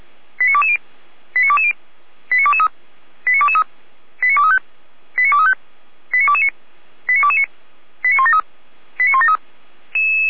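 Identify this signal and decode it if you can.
Five-tone selective call (ZVEI), heard as quick runs of five tones, tones of about 70 ms. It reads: 89220 89220 89223 89223 89235 89235 89229 89229 89173 89173
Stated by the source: ZVEI five-tone selective call received on SDR.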